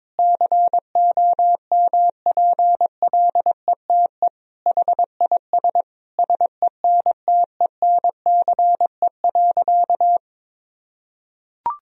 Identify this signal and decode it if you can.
Morse code, heard as a keyed tone, about 22 wpm, 700 Hz. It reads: COMPLETE HIS SENTENCE.
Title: Sentences from Top 1000 Words - Review Part 1 - 22wpm